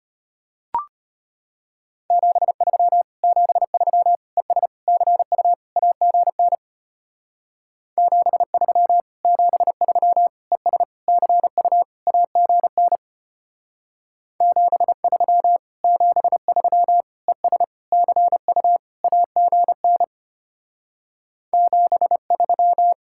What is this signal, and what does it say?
Morse code, heard as a keyed tone, about 31 wpm, 700 Hz. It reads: 73 73 ES CU AGN 73 73 ES CU AGN 73 73 ES CU AGN 73
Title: Speed-Racing Common QSO Phrases - 25wpm